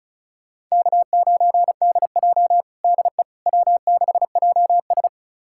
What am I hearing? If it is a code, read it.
Morse code, heard as a keyed tone, about 36 wpm, 700 Hz. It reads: K9DJ DE W6JS
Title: USA Callsign DE Callsign - 4 Characters - 35wpm